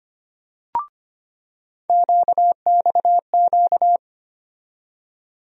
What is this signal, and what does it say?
Morse code, heard as a keyed tone, about 25 wpm, 700 Hz. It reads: QXQ
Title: USA Call Sign Postfixes - 3 Letters 25wpm